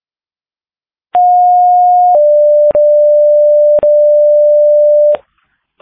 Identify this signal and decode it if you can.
Two-tone page, heard as a one-second tone, then a three-second tone, about 720 then 590 Hz. It is heard from 1.1 to 5.2 s.